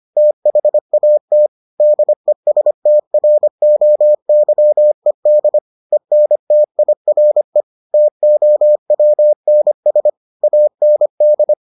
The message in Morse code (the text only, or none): THAT DESTROYED ENTIRE TOWNS AND